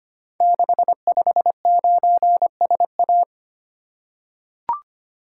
Morse code, heard as a keyed tone, about 25 wpm, 700 Hz. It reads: 659SA
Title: Code Groups - 5 Letter:Numbers - 25wpm